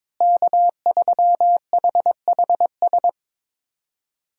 Morse code, heard as a keyed tone, about 22 wpm, 700 Hz. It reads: K3HHS
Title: USA Call Signs - 5 Characters - Repeated 3x - 22wpm